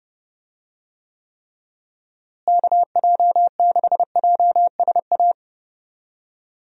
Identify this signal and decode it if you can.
Morse code, heard as a keyed tone, about 30 wpm, 700 Hz. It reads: KJ6JSA
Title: USA Call Signs - 6 Characters - Repeated 3x - 30wpm